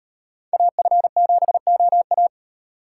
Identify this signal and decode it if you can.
Morse code, heard as a keyed tone, about 38 wpm, 700 Hz. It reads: AF7OA